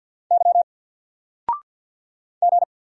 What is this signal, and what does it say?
Morse code, heard as a keyed tone, about 48 wpm, 700 Hz. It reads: Y G
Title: Single Character (Letter:Number:Punctuation:Prosign) 50wpm